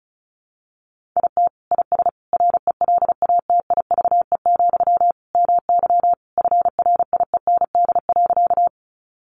Morse code, heard as a keyed tone, about 34 wpm, 700 Hz. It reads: IT IS RELATIVE, MY FRIEND.
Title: Sentences from Top 1400 Words 35wpm